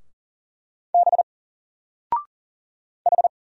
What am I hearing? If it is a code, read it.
Morse code, heard as a keyed tone, about 40 wpm, 700 Hz. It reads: B H
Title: Single Letter-Number 40wpm